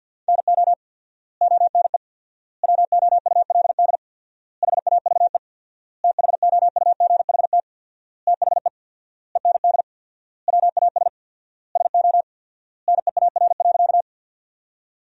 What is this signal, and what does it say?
Morse code, heard as a keyed tone, about 50 wpm, 700 Hz. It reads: NO ONE WOULD HAVE THOUGHT THE END WAS SO NEAR.
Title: 4x Word-Spacing - Sentences from Top 300 Words - 50wpm